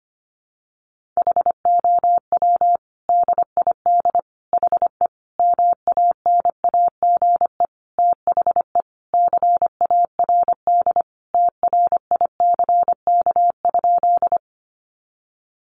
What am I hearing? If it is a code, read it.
Morse code, heard as a keyed tone, about 25 wpm, 700 Hz. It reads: HOW DID HE MANAGE THE CARD TRICK?
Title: Sentences from Top 1500 Words 25wpm